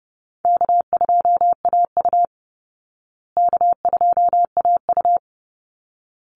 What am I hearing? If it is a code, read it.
Morse code, heard as a keyed tone, about 28 wpm, 700 Hz. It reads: K2AU K2AU
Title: Call Signs Repeated 3x - 30wpm